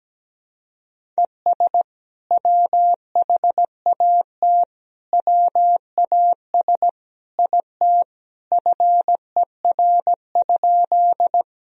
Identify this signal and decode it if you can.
Morse code, heard as a keyed tone, about 17 wpm, 700 Hz. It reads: ES WHAT WAS IT FER?
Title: Sentences from Top 100 Words and Abbreviations 17wpm